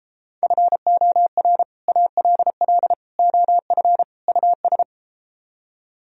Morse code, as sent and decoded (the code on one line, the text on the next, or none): ..-. --- .-. / .- .-.. .-.. / --- ..-. / ..- ...
FOR ALL OF US